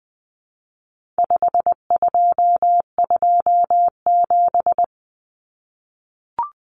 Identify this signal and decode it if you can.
Morse code, heard as a keyed tone, about 20 wpm, 700 Hz. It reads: 5227
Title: Common Cut Numbers - 4-Digit 20wpm